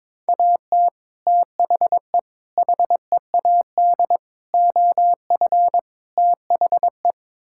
Morse code, heard as a keyed tone, about 22 wpm, 700 Hz. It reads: AT THE HEAD OF THE
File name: practice / 22wpm / Build A Phrase - 5-Word Phrases - 22wpm